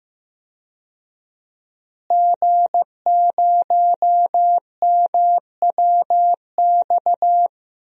Morse code, heard as a keyed tone, about 15 wpm, 700 Hz. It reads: G0MWX
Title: DXCC-100 Call Signs - 5 Characters - Repeated 2x - 15wpm